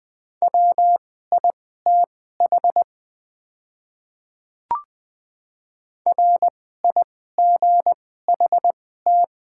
Morse code, heard as a keyed tone, about 20 wpm, spaced as for 14 wpm, 700 Hz. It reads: WITH RIGHT
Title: 2x Character Spacing - Top 300 Words - 20wpm